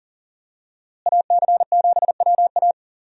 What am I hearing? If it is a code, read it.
Morse code, heard as a keyed tone, about 40 wpm, 700 Hz. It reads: AC7WA
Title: USA Call Signs - 5 Characters - Repeated 2x - 40wpm